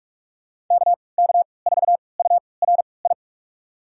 Morse code, heard as a keyed tone, about 45 wpm, spaced as for 25 wpm, 700 Hz. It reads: KK4URI
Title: USA Call Signs - 6 Characters - 3x Character Spacing - 45wpm